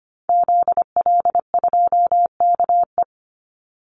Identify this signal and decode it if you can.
Morse code, heard as a keyed tone, about 25 wpm, 700 Hz. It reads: ZL2KE
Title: Call Signs Repeated 3x - 25wpm